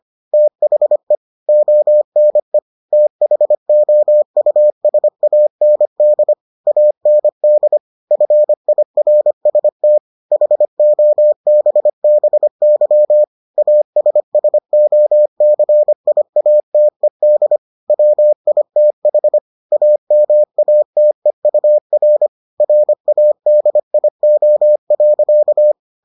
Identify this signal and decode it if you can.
Morse code, heard as a keyed tone, about 25 wpm, 600 Hz. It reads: THE ONE THOUSAND AND FIRST HOBBY ASSOCIATED WITH AMATEUR RADIO.